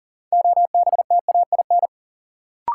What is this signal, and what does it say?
Morse code, heard as a keyed tone, about 40 wpm, 700 Hz. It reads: OBTAIN
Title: Sentences from Top 1300 Words - New Word Drill - 40wpm